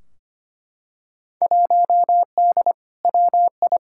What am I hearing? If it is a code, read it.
Morse code, heard as a keyed tone, about 25 wpm, 700 Hz. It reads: 1D WI
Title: Field Day Exchanges - 25wpm